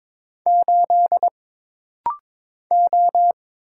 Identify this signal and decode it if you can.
Morse code, heard as a keyed tone, about 22 wpm, 700 Hz. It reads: 8 O
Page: This practice set is composed of Letters, Numbers, common punctuations, and common prosigns.